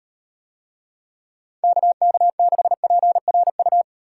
Morse code, heard as a keyed tone, about 37 wpm, 700 Hz. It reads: KK6PRU